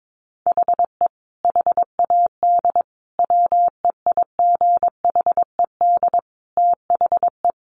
Morse code, heard as a keyed tone, about 22 wpm, 700 Hz. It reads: HE HAD WEIGHED THE
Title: A Sound Of Thunder 22wpm